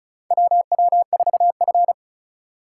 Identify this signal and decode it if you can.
Morse code, heard as a keyed tone, about 36 wpm, 700 Hz. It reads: WW4F
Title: USA Call Signs - 4 Characters - 3x Character Spacing - 35wpm